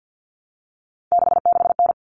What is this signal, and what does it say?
Morse code, heard as a keyed tone, about 48 wpm, 700 Hz. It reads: 66N
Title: All Cut Numbers - 3-Digit 50wpm